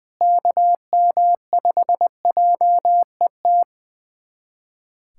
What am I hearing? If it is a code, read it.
Morse code, heard as a keyed tone, about 20 wpm, 700 Hz. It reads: KM5JET
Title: USA Callsign DE Callsign - 6 Characters - 20wpm